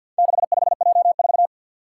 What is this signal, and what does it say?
Morse code, heard as a keyed tone, about 48 wpm, 700 Hz. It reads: 65J4
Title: Four Challenging Letter-Number Combinations 50wpm